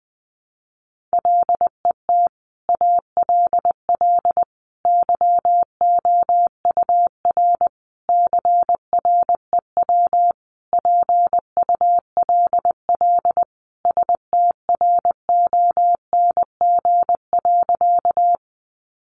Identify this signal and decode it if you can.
Morse code, heard as a keyed tone, about 20 wpm, 700 Hz. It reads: LET ALL YOUR CREW PULL STRONG.
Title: Sentences from Top 1500 Words 20wpm